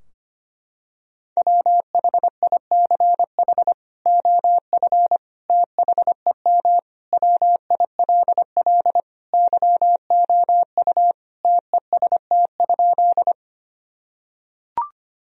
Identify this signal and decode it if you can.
Morse code, heard as a keyed tone, about 25 wpm, 700 Hz. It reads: WHICH OF THEM WILL YOU TEST?
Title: Sentences from Top 1000 Words - Review Part 3 - 25wpm